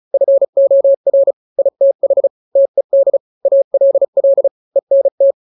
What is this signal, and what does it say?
Morse code, heard as a keyed tone, about 34 wpm, 550 Hz. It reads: FOR ITH TED ALL ENT
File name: 3-grams 35 30 WPM